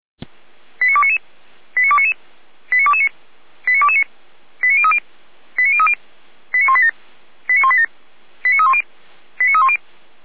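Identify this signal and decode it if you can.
Five-tone selective call (ZVEI), heard as quick runs of five tones, tones of about 70 ms. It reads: 89299 89299 89229 89229 89030 89030 89187 89187 89310 89310
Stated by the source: ZVEI five-tone call 89250 received by SDR, channel 455, recorded by feuerwehr-orlamuende fire brigade server.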